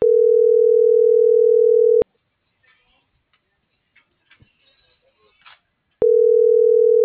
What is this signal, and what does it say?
Ringback tone (North America), 440 + 480 Hz.